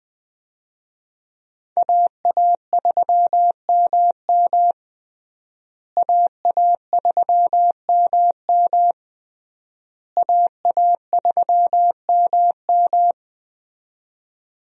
Morse code, heard as a keyed tone, about 20 wpm, 700 Hz. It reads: AA3MM AA3MM AA3MM
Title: USA Call Signs - 5 Characters - Repeated 3x - 20wpm